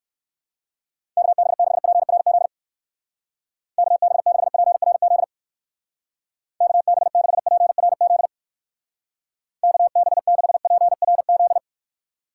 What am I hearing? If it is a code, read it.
Morse code, heard as a keyed tone, about 58 wpm, 700 Hz. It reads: KB6PRZ KB6PRZ KB6PRZ KB6PRZ